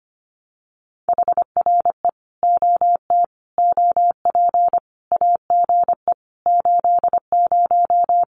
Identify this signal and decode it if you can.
Morse code, heard as a keyed tone, about 25 wpm, 700 Hz. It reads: HRE OT OP AGE 80
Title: Common QSO Phrases 25wpm